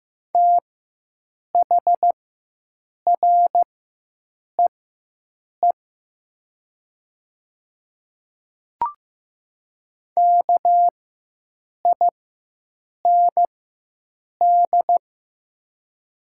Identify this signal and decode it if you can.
Morse code, heard as a keyed tone, about 15 wpm, spaced as for 7 wpm, 700 Hz. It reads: THREE KIND
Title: 4x Character Spacing - Top 300 Words - 15wpm